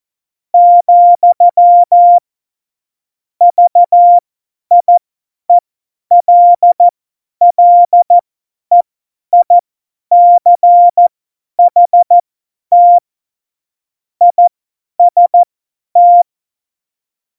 Morse code, heard as a keyed tone, about 14 wpm, spaced as for 10 wpm, 700 Hz. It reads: , VIELLEICHT IST
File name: Morsekurs / Witze / Witz 04. September 2018, 50 F70